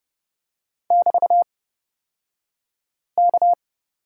Morse code, heard as a keyed tone, about 31 wpm, spaced as for 5 wpm, 700 Hz. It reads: <BT>K
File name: Lesson 36d - BT - Two Characters 30wpm